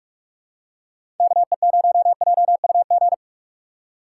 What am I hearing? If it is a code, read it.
Morse code, heard as a keyed tone, about 41 wpm, 700 Hz. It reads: KE0JUG
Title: USA Call Signs - 6 Characters - 2x Character Spacing - 45wpm